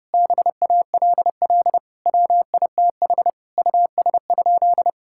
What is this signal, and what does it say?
Morse code, heard as a keyed tone, about 30 wpm, 700 Hz. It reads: BALL WITH US?
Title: Sentences from Top 1000 Words - Review Part 6 - 30wpm